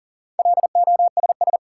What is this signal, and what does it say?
Morse code, heard as a keyed tone, about 40 wpm, 700 Hz. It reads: LOSS
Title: Sentences from Top 1400 Words - New Word Drill - 40wpm